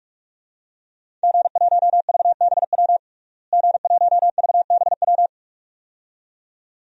Morse code, heard as a keyed tone, about 41 wpm, 700 Hz. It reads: G1VBW G1VBW
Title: DXCC-100 Call Signs - 5 Characters - Repeated 2x - 45wpm